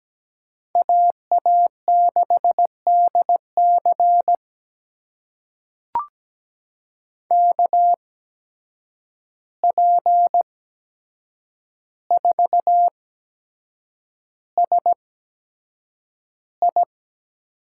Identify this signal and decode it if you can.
Morse code, heard as a keyed tone, about 17 wpm, spaced as for 7 wpm, 700 Hz. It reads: AA6DC KP4SI
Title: USA Call Signs - 8x Character Spacing - 17wpm